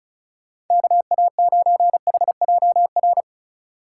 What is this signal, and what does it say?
Morse code, heard as a keyed tone, about 36 wpm, 700 Hz. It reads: KA9HJR
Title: USA Call Signs - 6 Characters - 8x Character Spacing - 35wpm